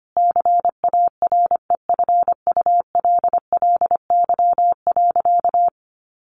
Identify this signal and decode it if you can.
Morse code, heard as a keyed tone, about 25 wpm, 700 Hz. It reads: CAREFULLY.
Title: Sentences from Top 1300 Words 25wpm